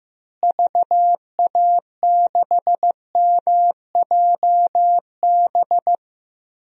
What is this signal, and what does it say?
Morse code, heard as a keyed tone, about 15 wpm, 700 Hz. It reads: VA6MJB
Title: DXCC-100 Call Signs - 6 Characters - Repeated 2x - 15wpm